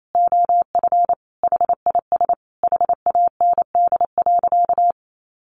Morse code, heard as a keyed tone, about 28 wpm, 700 Hz. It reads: OF HIS HAND.
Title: Sentences from Top 1400 Words 28wpm